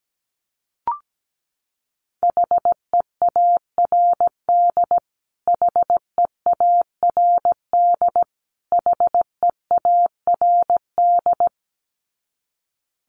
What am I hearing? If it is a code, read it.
Morse code, heard as a keyed tone, about 17 wpm, 700 Hz. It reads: HEARD HEARD HEARD
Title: Top 500 Words Repeated 17wpm